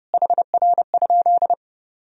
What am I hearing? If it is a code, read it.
Morse code, heard as a keyed tone, about 30 wpm, 700 Hz. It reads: HR?
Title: Sentences from Top 500 Words and Abbreviations 30wpm